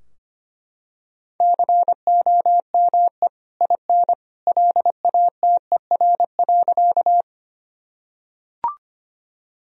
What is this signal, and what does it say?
Morse code, heard as a keyed tone, about 25 wpm, 700 Hz. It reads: COME IN LATER.
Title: Sentences from Top 1000 Words - Review Part 4 - 25wpm